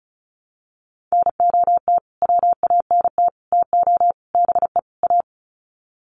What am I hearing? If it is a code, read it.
Morse code, heard as a keyed tone, about 34 wpm, 700 Hz. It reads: NOT WANT TO BE A
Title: Sets of 5 Words Encore 35wpm